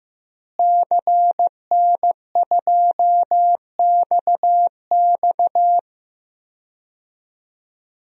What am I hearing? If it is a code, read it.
Morse code, heard as a keyed tone, about 15 wpm, 700 Hz. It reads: CN2XX
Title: DXCC-100 Call Signs - 5 Characters - Repeated 3x - 15wpm